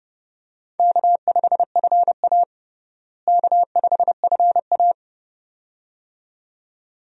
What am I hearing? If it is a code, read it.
Morse code, heard as a keyed tone, about 29 wpm, 700 Hz. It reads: K5FA K5FA
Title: USA Call Signs - 4 Characters - Repeated 2x - 30wpm